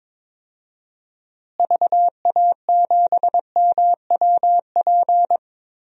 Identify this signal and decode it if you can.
Morse code, heard as a keyed tone, about 22 wpm, 700 Hz. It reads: VA7MWP
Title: DXCC-100 Call Signs - 6 Characters - Repeated 2x - 22wpm